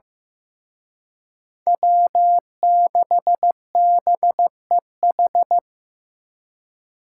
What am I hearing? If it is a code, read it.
Morse code, heard as a keyed tone, about 18 wpm, 700 Hz. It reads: W6BEH
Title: Speed Racing - USA 1x3 Call Signs 15wpm